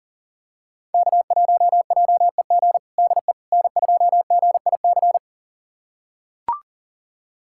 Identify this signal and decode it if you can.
Morse code, heard as a keyed tone, about 37 wpm, 700 Hz. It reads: K1JEG DE N2GIC
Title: USA Callsign DE Callsign 40wpm